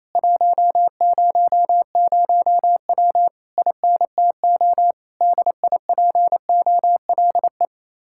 Morse code, heard as a keyed tone, about 26 wpm, 700 Hz. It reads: 100W INTO DIPOLE
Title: Speed-Racing Common QSO Phrases - 28wpm